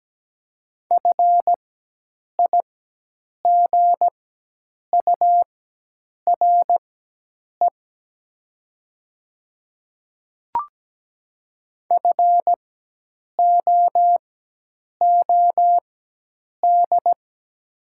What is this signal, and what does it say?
Morse code, heard as a keyed tone, about 17 wpm, spaced as for 8 wpm, 700 Hz. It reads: FIGURE FOOD
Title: 4x Character Spacing - Top 500 Words - 17wpm